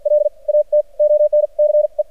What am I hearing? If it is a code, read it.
Morse code, heard as a keyed tone, about 53 wpm, spaced as for 39 wpm, 600 Hz. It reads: LAT ONCE
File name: archived / CRSnet